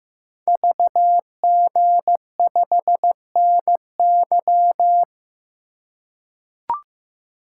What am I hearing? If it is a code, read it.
Morse code, heard as a keyed tone, about 15 wpm, 700 Hz. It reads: VG5NY
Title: Code Groups - 5 Letter:Numbers - 15wpm